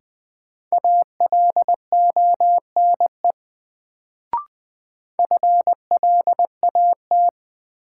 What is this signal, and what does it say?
Morse code, heard as a keyed tone, about 20 wpm, 700 Hz. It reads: ALONE FLAT
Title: Sentences from Top 1000 Words - New Word Drill - 20wpm